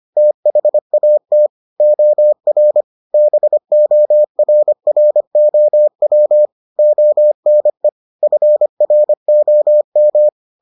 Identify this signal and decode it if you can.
Morse code, heard as a keyed tone, about 25 wpm, 600 Hz. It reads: THAT OR BORROW ONE FROM